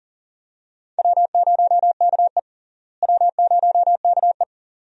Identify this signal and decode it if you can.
Morse code, heard as a keyed tone, about 40 wpm, 700 Hz. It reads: W0KE W0KE
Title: USA Call Signs - 4 Characters - Repeated 2x - 40wpm